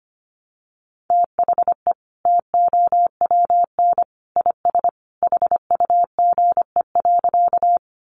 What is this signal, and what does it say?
Morse code, heard as a keyed tone, about 25 wpm, 700 Hz. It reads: THE TOWN IS HUGE.
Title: Sentences from Top 1100 Words 25wpm